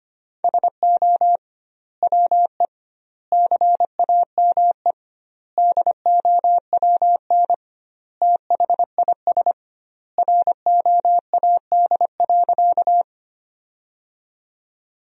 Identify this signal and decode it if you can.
Morse code, heard as a keyed tone, about 25 wpm, 700 Hz. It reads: SO WE CAME DOWN THIS ROAD.
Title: 2x Word-Spacing - Sentences from Top 400 Words - 25wpm